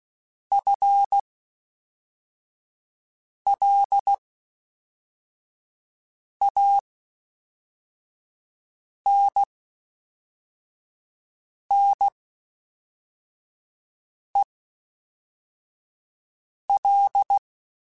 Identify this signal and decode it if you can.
Morse code, heard as a keyed tone, about 16 wpm, spaced as for 4 wpm, 800 Hz. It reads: FLANNEL
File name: Spaced / FL Runs sp mp3s / FL Runs sp @5